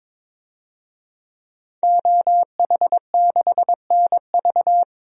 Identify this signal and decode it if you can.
Morse code, heard as a keyed tone, about 22 wpm, 700 Hz. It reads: OH6NV
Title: DXCC-100 Call Signs - 5 Characters - 22wpm